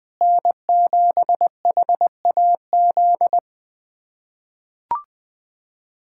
Morse code, heard as a keyed tone, about 20 wpm, 700 Hz. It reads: N7HAZ
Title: USA Callsign DE Callsign - 5 Characters - 20wpm